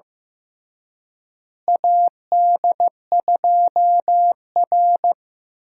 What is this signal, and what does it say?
Morse code, heard as a keyed tone, about 15 wpm, 700 Hz. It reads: AD2R
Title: USA Call Signs - 4 Characters - Repeated 3x - 15wpm